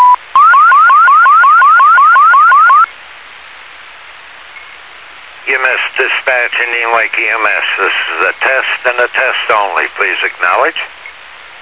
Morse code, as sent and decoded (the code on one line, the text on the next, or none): none